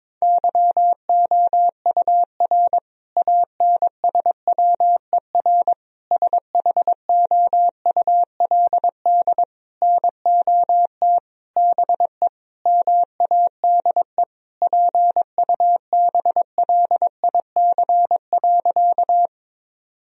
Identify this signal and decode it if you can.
Morse code, heard as a keyed tone, about 22 wpm, 700 Hz. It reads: YOUR ANSWER SHOULD NOT BE MADE PUBLIC.
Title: Sentences from Top 1000 Words - Review Part 3 - 22wpm